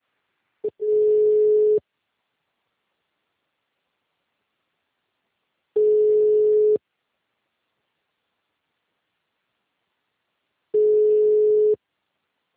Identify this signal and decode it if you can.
Ringback tone (Europe, 425 Hz).